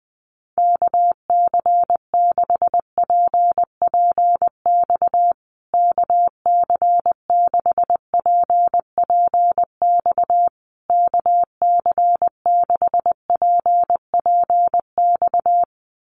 Morse code, heard as a keyed tone, about 20 wpm, 700 Hz. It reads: KC6PPX KC6PPX KC6PPX
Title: USA Call Signs Repeated 3x 20wpm